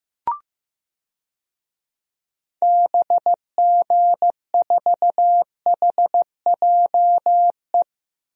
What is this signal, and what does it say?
Morse code, heard as a keyed tone, about 15 wpm, 700 Hz. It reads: BG4HJE
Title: DXCC-100 Call Signs - 6 Characters - Repeated 3x - 15wpm